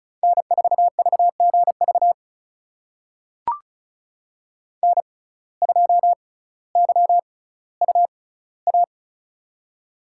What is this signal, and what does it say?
Morse code, heard as a keyed tone, about 35 wpm, spaced as for 18 wpm, 700 Hz. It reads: N4VGV N2YUA
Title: USA Call Signs - 5 Characters - 6x Character Spacing - 35wpm